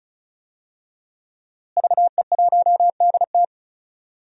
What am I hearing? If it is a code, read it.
Morse code, heard as a keyed tone, about 36 wpm, 700 Hz. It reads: VE1DT
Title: DXCC-100 Call Signs - 5 Characters - Repeated 3x - 35wpm